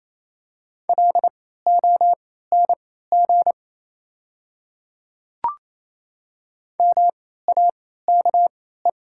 Morse code, heard as a keyed tone, about 27 wpm, spaced as for 16 wpm, 690 Hz. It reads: LONG MAKE
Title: 3x Character Spacing - Top 100 Words - 28wpm